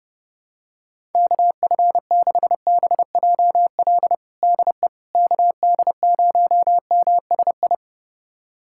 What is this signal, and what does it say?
Morse code, heard as a keyed tone, about 29 wpm, 700 Hz. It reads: KF6BJL DE KD0MSI
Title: USA Callsign DE Callsign - 6 Characters - 30wpm